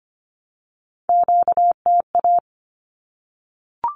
Morse code, heard as a keyed tone, about 25 wpm, 700 Hz. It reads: QTA
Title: Q-Codes 25wpm